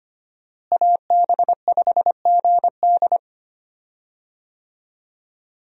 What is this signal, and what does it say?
Morse code, heard as a keyed tone, about 25 wpm, 700 Hz. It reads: AB5GD
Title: USA Call Signs - 5 Characters - Repeated 2x - 25wpm